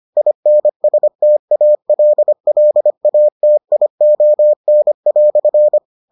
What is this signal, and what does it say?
Morse code, heard as a keyed tone, about 25 wpm, 600 Hz. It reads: INSTALLATION"